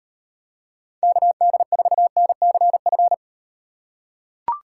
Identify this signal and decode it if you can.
Morse code, heard as a keyed tone, about 38 wpm, 700 Hz. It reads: KD4NCF